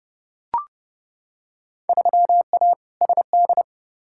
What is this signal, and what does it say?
Morse code, heard as a keyed tone, about 32 wpm, 700 Hz. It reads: 3A SD
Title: Field Day Exchanges - 30wpm